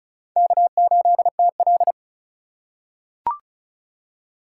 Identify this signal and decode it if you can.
Morse code, heard as a keyed tone, about 36 wpm, 700 Hz. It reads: K8TL
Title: USA Call Signs - 3x Character Spacing - 35wpm